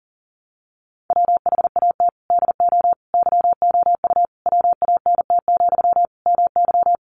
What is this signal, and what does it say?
Morse code, heard as a keyed tone, about 40 wpm, 700 Hz. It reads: WHAT DO YOU WANT, MY
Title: Sentences from Top 1000 Words 40wpm